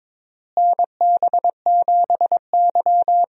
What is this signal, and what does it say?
Morse code, heard as a keyed tone, about 21 wpm, 700 Hz. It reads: NB7Y
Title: USA Call Signs - 4 Characters - Repeated 2x - 22wpm